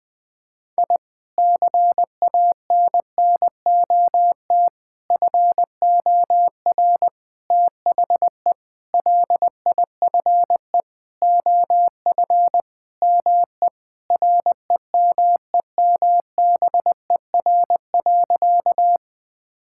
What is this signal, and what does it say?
Morse code, heard as a keyed tone, about 20 wpm, 700 Hz. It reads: I CANNOT FOR THE LIFE OF ME REMEMBER.
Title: Sentences with 8-Word Phrases - 20wpm